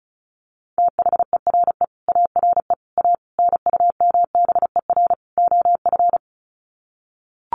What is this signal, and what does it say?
Morse code, heard as a keyed tone, about 34 wpm, 700 Hz. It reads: THERE ARE A NUMBER OF
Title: Sets of 5 Words 35wpm